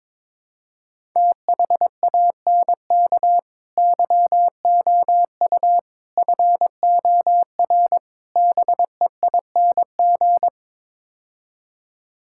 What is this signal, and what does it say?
Morse code, heard as a keyed tone, about 22 wpm, 700 Hz. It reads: THANK YOU FOR BEING
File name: practice / 22wpm / Build A Phrase - 5-Word Phrases - 22wpm